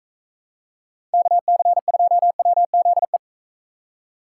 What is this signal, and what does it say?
Morse code, heard as a keyed tone, about 44 wpm, 700 Hz. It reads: KC2WZE